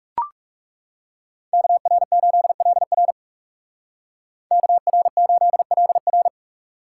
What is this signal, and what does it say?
Morse code, heard as a keyed tone, about 35 wpm, 700 Hz. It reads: KR8LR KR8LR
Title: Speed Racing - USA 2x2 Call Signs 30wpm